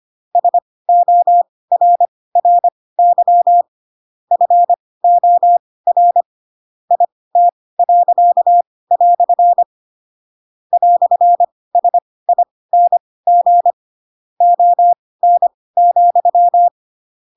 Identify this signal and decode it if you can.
Morse code, heard as a keyed tone, about 25 wpm, spaced as for 18 wpm, 700 Hz. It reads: SORRY FOR IT." "SING ON,
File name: Pinocchio / Pinocchio ch4 12 25 ik1hge